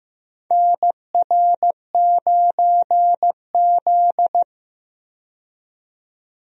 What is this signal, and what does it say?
Morse code, heard as a keyed tone, about 15 wpm, 700 Hz. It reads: NR9Z
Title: USA Call Signs - 4 Characters - 15wpm